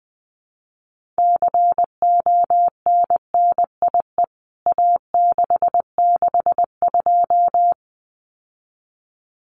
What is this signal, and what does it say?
Morse code, heard as a keyed tone, about 19 wpm, 700 Hz. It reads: CONNIE A662
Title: CWT Exchange 20wpm